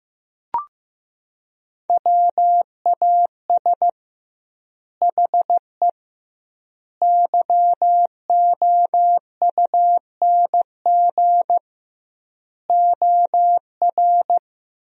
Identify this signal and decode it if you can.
Morse code, heard as a keyed tone, about 15 wpm, 700 Hz. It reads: WAS HE YOUNG OR
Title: 2x Word-Spacing - Sentences from Top 400 Words - 15wpm